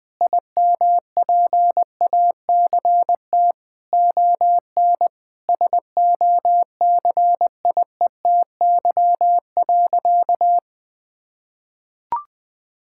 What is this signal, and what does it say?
Morse code, heard as a keyed tone, about 20 wpm, 700 Hz. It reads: IMPACT ON SOCIETY.
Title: Sentences from Top 1000 Words - Review Part 6 - 20wpm